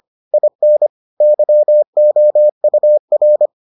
Morse code, heard as a keyed tone, about 25 wpm, 600 Hz. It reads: IN YOUR